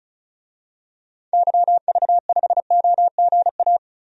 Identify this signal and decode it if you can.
Morse code, heard as a keyed tone, about 36 wpm, 700 Hz. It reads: YV5OGA
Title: DXCC-100 Call Signs - 6 Characters - 35wpm